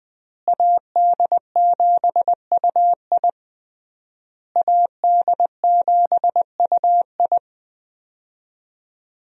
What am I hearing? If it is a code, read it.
Morse code, heard as a keyed tone, about 20 wpm, 700 Hz. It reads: AD7UI AD7UI
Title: USA Call Signs - 5 Characters - Repeated 3x - 20wpm